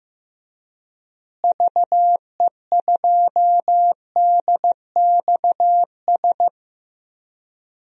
Morse code, heard as a keyed tone, about 15 wpm, 700 Hz. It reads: VE2DXS
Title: DXCC-100 Call Signs - 6 Characters - Repeated 3x - 15wpm